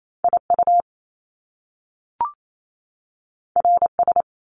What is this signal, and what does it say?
Morse code, heard as a keyed tone, about 28 wpm, 700 Hz. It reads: I U R S
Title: Top Two Letter Combinations in Words 28wpm